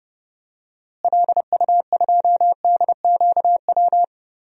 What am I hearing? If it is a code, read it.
Morse code, heard as a keyed tone, about 30 wpm, 700 Hz. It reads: LU2DQW
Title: DXCC-100 Call Signs - 6 Characters - Repeated 3x - 30wpm